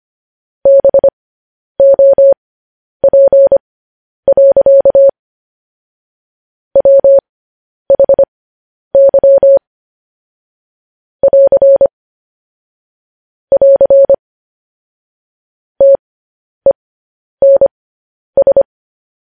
Morse code, heard as a keyed tone, about 25 wpm, spaced as for 10 wpm, 550 Hz. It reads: BOP. WHY <AR> <AR> TENS